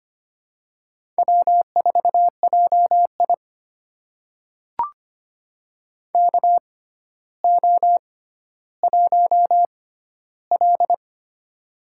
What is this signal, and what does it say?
Morse code, heard as a keyed tone, about 25 wpm, spaced as for 13 wpm, 700 Hz. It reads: W4JI KO1L
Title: USA Call Signs - 4 Characters - 6x Character Spacing - 25wpm